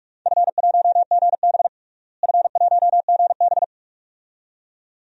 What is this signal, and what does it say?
Morse code, heard as a keyed tone, about 42 wpm, 700 Hz. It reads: F1GB F1GB
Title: DXCC-100 Call Signs - 4 Characters - Repeated 2x - 45wpm